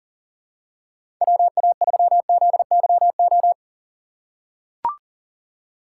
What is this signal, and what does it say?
Morse code, heard as a keyed tone, about 40 wpm, 700 Hz. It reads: WA3ZYO
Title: USA Call Signs - 6 Characters - 4x Character Spacing - 40wpm